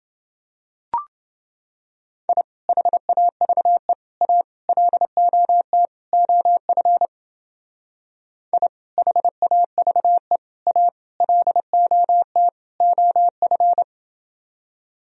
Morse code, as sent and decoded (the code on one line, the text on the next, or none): .. / .... .- ...- . / .- / .-.. --- - / --- ..-. / .. / .... .- ...- . / .- / .-.. --- - / --- ..-.
I HAVE A LOT OF I HAVE A LOT OF